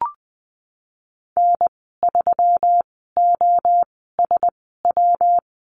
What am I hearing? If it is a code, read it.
Morse code, heard as a keyed tone, about 20 wpm, spaced as for 15 wpm, 700 Hz. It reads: N3OSW
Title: 147 Lesson 24g - Letter y - Call Signs 20wpm